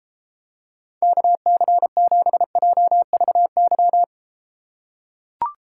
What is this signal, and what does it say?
Morse code, heard as a keyed tone, about 32 wpm, 700 Hz. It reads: KC7JVY